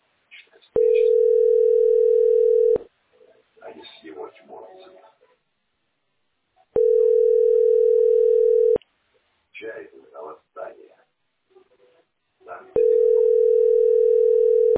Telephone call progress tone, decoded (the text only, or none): ringback tone